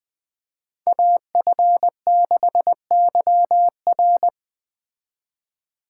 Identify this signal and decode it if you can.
Morse code, heard as a keyed tone, about 20 wpm, 700 Hz. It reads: AF6YR
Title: Speed Racing - USA Call Signs 17wpm